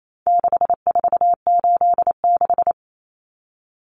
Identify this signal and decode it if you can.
Morse code, heard as a keyed tone, about 28 wpm, 700 Hz. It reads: 6486
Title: Numbers - 4-Digit 28wpm